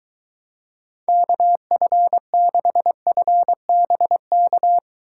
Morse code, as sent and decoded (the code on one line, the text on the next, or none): -.- ..-. -.... ..-. -... -.-
KF6FBK